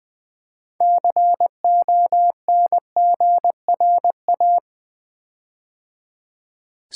Morse code, heard as a keyed tone, about 20 wpm, 700 Hz. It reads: CONGRA
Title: Build A Word - 6 Letters At A Time - 12-Letter Words - 20wpm